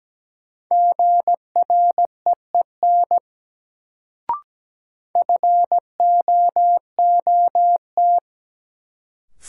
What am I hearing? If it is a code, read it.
Morse code, heard as a keyed tone, about 17 wpm, 700 Hz. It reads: GREEN FOOT
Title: Sentences from Top 500 Words - New Word Drill - 17wpm